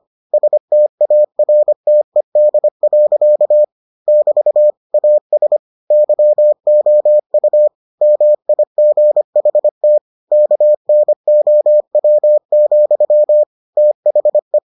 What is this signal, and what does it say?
Morse code, heard as a keyed tone, about 25 wpm, 600 Hz. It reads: STARTED. <BT> AS YOU MIGHT KNOW, THE